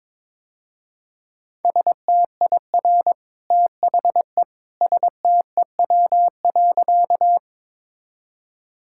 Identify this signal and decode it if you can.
Morse code, heard as a keyed tone, about 22 wpm, 700 Hz. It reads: STIR THE STEW.
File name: practice / 22wpm / Easy Sentences with Hints - 22wpm